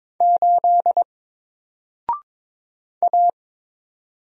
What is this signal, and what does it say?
Morse code, heard as a keyed tone, about 22 wpm, 700 Hz. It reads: 8 A
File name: Single-Letter-Number-22wpm